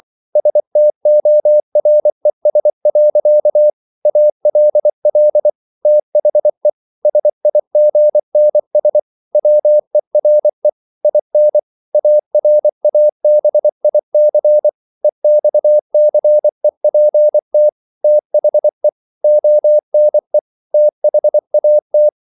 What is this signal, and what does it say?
Morse code, heard as a keyed tone, about 24 wpm, 600 Hz. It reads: STORES. ALL THE SIGNS WERE IN ARABIC EXCEPT THE ONE THAT